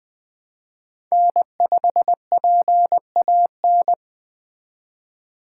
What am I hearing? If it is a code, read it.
Morse code, heard as a keyed tone, about 20 wpm, 700 Hz. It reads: N5PAN
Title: Speed Racing - USA 1x3 Call Signs 17wpm